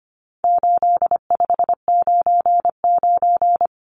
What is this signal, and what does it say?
Morse code, heard as a keyed tone, about 25 wpm, 700 Hz. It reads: 8599
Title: Numbers - 4-Digit 25wpm